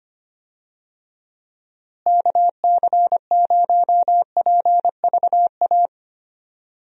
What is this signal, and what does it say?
Morse code, heard as a keyed tone, about 25 wpm, 700 Hz. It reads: KC0PVA
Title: USA Call Signs - 6 Characters - Repeated 2x - 25wpm